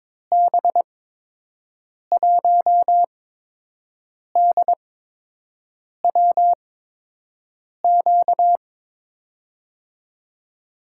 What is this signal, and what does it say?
Morse code, heard as a keyed tone, about 22 wpm, spaced as for 6 wpm, 700 Hz. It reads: B1DWQ
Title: USA Call Signs - 6 Characters - 8x Character Spacing - 22wpm